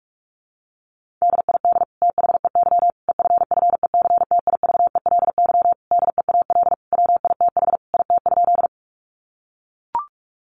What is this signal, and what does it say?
Morse code, heard as a keyed tone, about 44 wpm, 700 Hz. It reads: DID THEY EFFECTIVELY DEAL WITH IT?
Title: Sentences from Top 1400 Words 45wpm